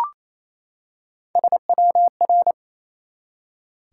Morse code, heard as a keyed tone, about 29 wpm, 700 Hz. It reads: SWR
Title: Top 180 QSO Elements 28wpm